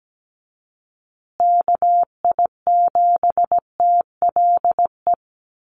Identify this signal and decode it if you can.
Morse code, heard as a keyed tone, about 17 wpm, 700 Hz. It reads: KI7TLE
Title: USA Call Signs Repeated 2x 17wpm